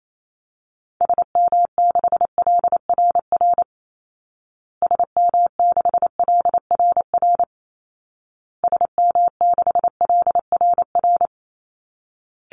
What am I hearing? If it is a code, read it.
Morse code, heard as a keyed tone, about 28 wpm, 700 Hz. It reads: SM6LRR SM6LRR SM6LRR
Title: Call Signs Repeated 2x - 28wpm